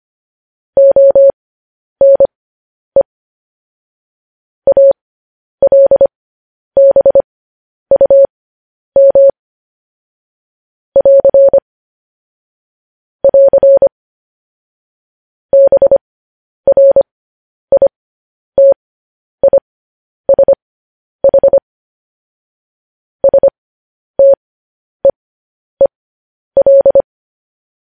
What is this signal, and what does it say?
Morse code, heard as a keyed tone, about 25 wpm, spaced as for 10 wpm, 550 Hz. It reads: ONE ALBUM <AR> <AR> BRITISH STEEL
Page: news headlines in morse code at 10 wpm